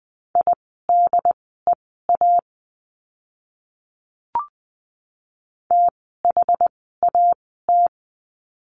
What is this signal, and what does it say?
Morse code, heard as a keyed tone, about 20 wpm, spaced as for 14 wpm, 700 Hz. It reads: IDEA THAT
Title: 108 Lesson 18f - Letter m - Words 20wpm